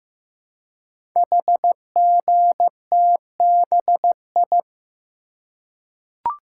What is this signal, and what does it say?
Morse code, heard as a keyed tone, about 15 wpm, 700 Hz. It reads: HGTBI
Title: Code Groups - 5 Letters - 15wpm